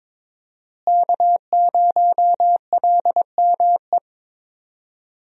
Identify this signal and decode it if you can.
Morse code, heard as a keyed tone, about 22 wpm, 700 Hz. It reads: K0LME
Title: USA Call Signs - 5 Characters - Repeated 2x - 22wpm